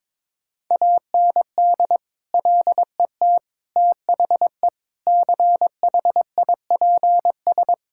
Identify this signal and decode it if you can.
Morse code, heard as a keyed tone, about 22 wpm, 700 Hz. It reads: AND LET THE CHIPS
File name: practice / 22wpm / Sets of 8 Words - 22wpm